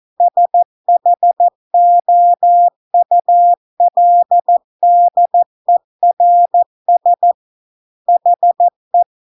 Morse code, heard as a keyed tone, about 14 wpm, 700 Hz. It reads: SHOULDERS HE